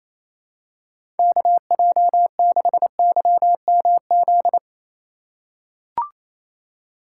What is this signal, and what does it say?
Morse code, heard as a keyed tone, about 28 wpm, 700 Hz. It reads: KJ6YMZ
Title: USA Call Signs - 6 Characters - 3x Character Spacing - 28wpm